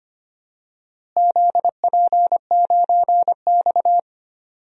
Morse code, heard as a keyed tone, about 25 wpm, 700 Hz. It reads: ZP9X
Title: DXCC-100 Call Signs - 4 Characters - Repeated 3x - 25wpm